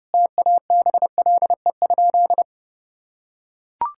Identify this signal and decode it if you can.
Morse code, heard as a keyed tone, about 32 wpm, 700 Hz. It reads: TABLE?
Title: Sentences from Top 1600 Words 30wpm